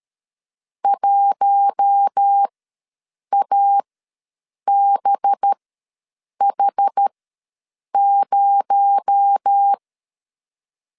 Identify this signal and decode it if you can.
Morse code, heard as a keyed tone, about 13 wpm, spaced as for 7 wpm, 800 Hz. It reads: 1ABH0
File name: LEZIONI / 7 GF9Z / 7 G ws 12wpm 3140